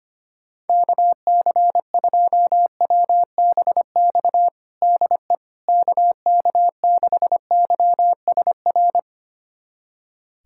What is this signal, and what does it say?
Morse code, heard as a keyed tone, about 25 wpm, 700 Hz. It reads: KC2WBX DE KK6YSR
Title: USA Callsign DE Callsign - 6 Characters - 25wpm